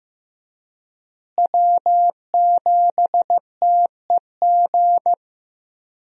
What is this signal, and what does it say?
Morse code, heard as a keyed tone, about 16 wpm, 700 Hz. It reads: W7TEG
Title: USA Call Signs - 5 Characters - Repeated 3x - 15wpm